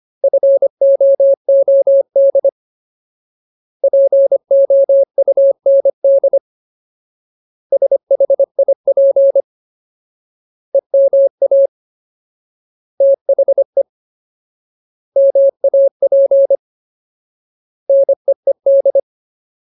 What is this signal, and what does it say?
Morse code, heard as a keyed tone, about 25 wpm, 550 Hz. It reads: FOOD POUND SHIP EMA THE MAP NEED
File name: Ultimate Word List / 25WPM Ultimate